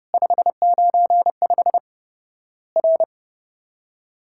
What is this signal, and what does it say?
Morse code, heard as a keyed tone, about 30 wpm, 700 Hz. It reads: 595 R
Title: Worked All Europe DX Contest - QTC Exchange - 30wpm